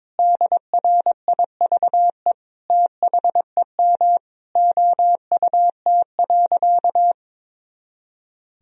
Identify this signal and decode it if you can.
Morse code, heard as a keyed tone, about 21 wpm, 700 Hz. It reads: DRIVE THEM OUT.
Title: Sentences from Top 500 Words and Abbreviations 22wpm